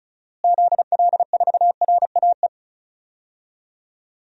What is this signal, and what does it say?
Morse code, heard as a keyed tone, about 36 wpm, 700 Hz. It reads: ZL4RAE
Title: DXCC-100 Call Signs - 6 Characters - Repeated 3x - 35wpm